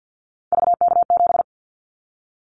Morse code, heard as a keyed tone, about 50 wpm, 700 Hz. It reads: VK7
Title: Callsign Prefixes for Top 100 DXCC - 50wpm